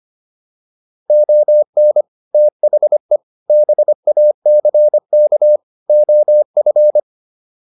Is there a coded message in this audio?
Yes: Morse code from 1.1 to 7.0 s, heard as a keyed tone, about 25 wpm, 600 Hz. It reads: ON THE BACK OF